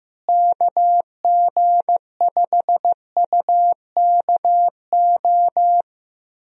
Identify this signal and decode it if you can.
Morse code, heard as a keyed tone, about 15 wpm, 700 Hz. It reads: KG5UKO
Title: USA Call Signs - 6 Characters - Repeated 3x - 15wpm